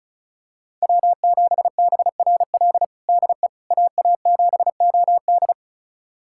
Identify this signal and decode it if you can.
Morse code, heard as a keyed tone, about 36 wpm, 700 Hz. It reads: W7BRL DE AA7OD
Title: USA Callsign DE Callsign - 5 Characters - 35wpm